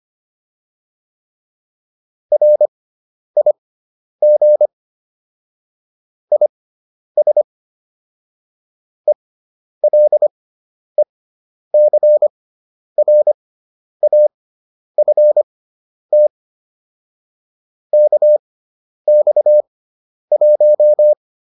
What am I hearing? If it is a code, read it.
Morse code, heard as a keyed tone, about 25 wpm, spaced as for 10 wpm, 600 Hz. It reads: RIG IS ELECRAFT KX1